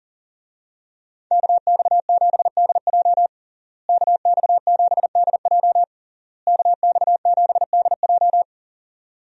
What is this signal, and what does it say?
Morse code, heard as a keyed tone, about 37 wpm, 700 Hz. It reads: KX7DJ KX7DJ KX7DJ
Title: USA Call Signs - 5 Characters - Repeated 3x - 40wpm